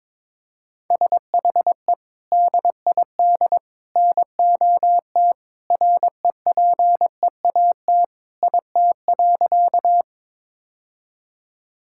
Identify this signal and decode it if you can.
Morse code, heard as a keyed tone, about 22 wpm, 700 Hz. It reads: SHE DID NOT REPEAT IT.
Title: Sentences from Top 1600 Words 22wpm